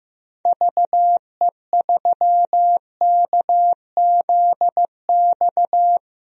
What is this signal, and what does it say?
Morse code, heard as a keyed tone, about 15 wpm, 700 Hz. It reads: VE3KZX
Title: DXCC-100 Call Signs - 6 Characters - Repeated 3x - 15wpm